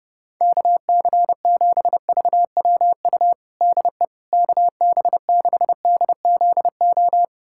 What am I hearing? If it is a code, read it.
Morse code, heard as a keyed tone, about 30 wpm, 700 Hz. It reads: KC7VWU DE KB6DZO
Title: USA Callsign DE Callsign 30wpm